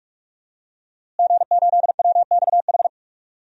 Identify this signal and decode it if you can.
Morse code, heard as a keyed tone, about 43 wpm, 700 Hz. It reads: G8WXH
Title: DXCC-100 Call Signs - 5 Characters - Repeated 2x - 45wpm